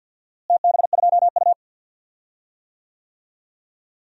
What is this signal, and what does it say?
Morse code, heard as a keyed tone, about 48 wpm, 700 Hz. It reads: TB2U
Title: DXCC-100 Call Signs - 4 Characters - Repeated 2x - 50wpm